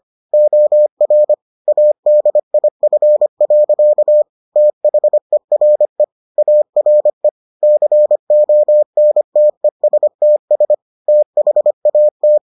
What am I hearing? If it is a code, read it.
Morse code, heard as a keyed tone, about 25 wpm, 600 Hz. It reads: OR ADIF. THERE ARE CONTESTS THAT